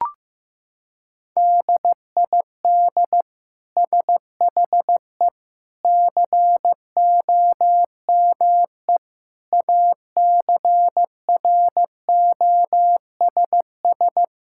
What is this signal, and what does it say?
Morse code, heard as a keyed tone, about 15 wpm, 700 Hz. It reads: DID SHE COME ACROSS
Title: Sentences from Top 1000 Words - Review Part 5 - 15wpm